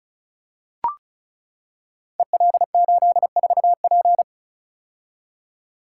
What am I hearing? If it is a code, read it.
Morse code, heard as a keyed tone, about 36 wpm, 690 Hz. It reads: EL84P
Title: Code Groups - 5 Letter:Numbers - 35wpm